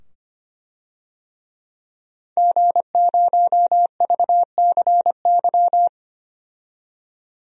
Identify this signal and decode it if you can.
Morse code, heard as a keyed tone, about 25 wpm, 700 Hz. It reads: G0VCY
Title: DXCC-100 Call Signs - 5 Characters - 25wpm